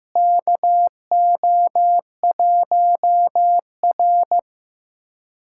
Morse code, heard as a keyed tone, about 15 wpm, 700 Hz. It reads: KO1R
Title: USA Callsign DE Callsign - 4 Characters - 15wpm